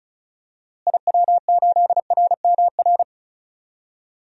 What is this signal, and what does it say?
Morse code, heard as a keyed tone, about 36 wpm, 700 Hz. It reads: IW8RMR
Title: DXCC-100 Call Signs - 6 Characters - Repeated 3x - 35wpm